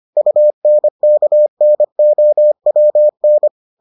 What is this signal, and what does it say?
Morse code, heard as a keyed tone, about 25 wpm, 600 Hz. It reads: UNKNOWN